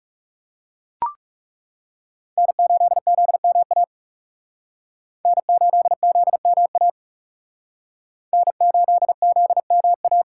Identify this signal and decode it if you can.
Morse code, heard as a keyed tone, about 37 wpm, 700 Hz. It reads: N8ZMA N8ZMA N8ZMA
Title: Speed Racing - USA 1x3 Call Signs 30wpm